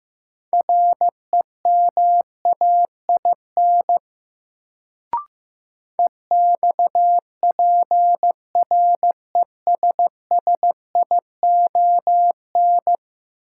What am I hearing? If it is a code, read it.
Morse code, heard as a keyed tone, about 16 wpm, 700 Hz. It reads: REMAIN EXPRESSION
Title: Sentences from Top 1200 Words - New Word Drill Alt - 15wpm